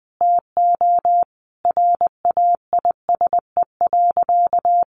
Morse code, heard as a keyed tone, about 20 wpm, 700 Hz. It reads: TO RAISE.
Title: Sentences from Top 1400 Words 20wpm